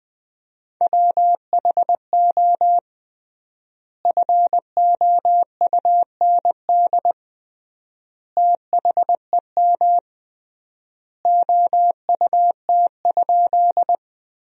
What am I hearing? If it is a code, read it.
Morse code, heard as a keyed tone, about 20 wpm, 700 Hz. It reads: WHO FOUND THEM OUT?
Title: 3x Word-Spacing - Sentences from Top 300 Words - 20wpm